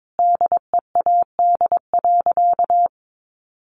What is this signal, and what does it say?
Morse code, heard as a keyed tone, about 22 wpm, 700 Hz. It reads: DEAD.
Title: Sentences from Top 1500 Words 22wpm